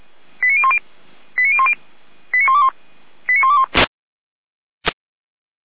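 Five-tone selective call (ZVEI), heard as quick runs of five tones, tones of about 70 ms. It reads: 89010 89010 89121 89121
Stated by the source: ZVEI five-tone selective call received on SDR.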